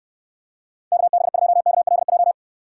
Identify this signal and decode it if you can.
Morse code, heard as a keyed tone, about 66 wpm, 700 Hz. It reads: KB2KLJ